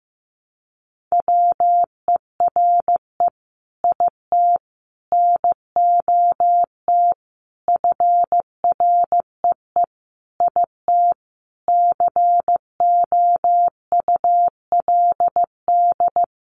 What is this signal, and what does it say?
Morse code, heard as a keyed tone, about 15 wpm, 700 Hz. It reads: WERE IT NOT FREE IT COULD
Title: Sentences from Top 900 Words 15wpm